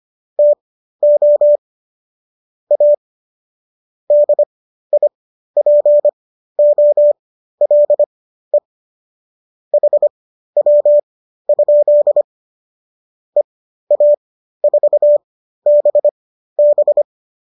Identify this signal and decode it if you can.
Morse code, heard as a keyed tone, about 25 wpm, spaced as for 13 wpm, 590 Hz. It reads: TO A DIPOLE HW? EA4BB